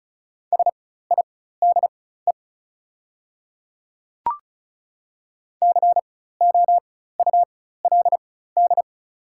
Morse code, heard as a keyed tone, about 36 wpm, spaced as for 16 wpm, 690 Hz. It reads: SIDE COULD
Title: 4x Character Spacing - Top 200 Words - 35wpm